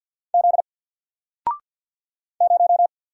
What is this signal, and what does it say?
Morse code, heard as a keyed tone, about 54 wpm, 700 Hz. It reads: Z 0
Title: Single Letter-Number Alt 50wpm